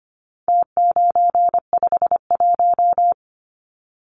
Morse code, heard as a keyed tone, about 25 wpm, 700 Hz. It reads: T951
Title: Numbers - Contest Style - 4-Digit 25wpm